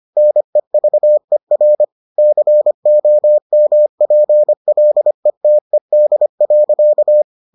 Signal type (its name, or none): Morse code